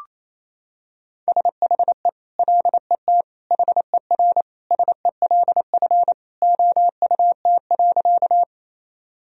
Morse code, heard as a keyed tone, about 28 wpm, 700 Hz. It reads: SHE LET HER SELF OUT.
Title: Sentences from Top 1000 Words - Review Part 1 - 28wpm